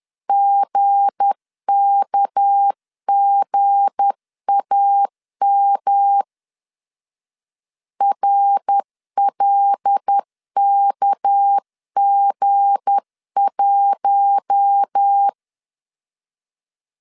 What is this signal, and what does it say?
Morse code, heard as a keyed tone, about 10 wpm, 800 Hz. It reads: GKGAM RLKG1
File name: LEZIONI / 7 GF9Z / 7 G 10wpm 2501